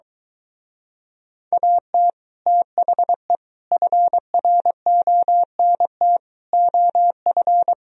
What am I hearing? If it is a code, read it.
Morse code, heard as a keyed tone, about 23 wpm, 700 Hz. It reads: AT THE FRONT OF